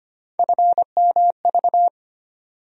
Morse code, heard as a keyed tone, about 25 wpm, 700 Hz. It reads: FMV
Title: USA Call Sign Postfixes - 3 Letters 25wpm